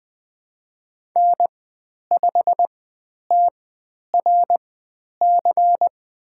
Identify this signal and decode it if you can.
Morse code, heard as a keyed tone, about 20 wpm, spaced as for 10 wpm, 700 Hz. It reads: N5TRC
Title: 102 Lesson 17g - Letter u - Call Signs 20wpm